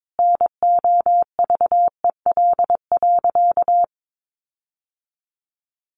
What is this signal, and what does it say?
Morse code, heard as a keyed tone, about 22 wpm, 700 Hz. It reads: NOVEL.
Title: Sentences from Top 1300 Words 22wpm